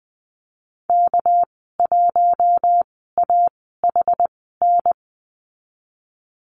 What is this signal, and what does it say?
Morse code, heard as a keyed tone, about 20 wpm, spaced as for 15 wpm, 700 Hz. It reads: K1AHN
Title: 245 Lesson 39g - Letter z - Call Signs 20wpm